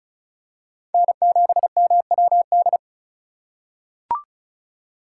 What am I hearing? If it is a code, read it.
Morse code, heard as a keyed tone, about 34 wpm, 690 Hz. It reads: N7MWD
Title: USA Call Signs - 5 Characters - 2x Character Spacing - 35wpm